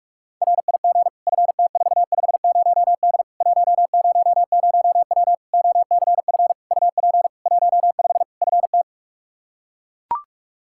Morse code, heard as a keyed tone, about 42 wpm, 700 Hz. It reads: RIG FT450D 100W OCF UP 15 FT
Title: Practice QSO - Exchange 2 - Info - 45wpm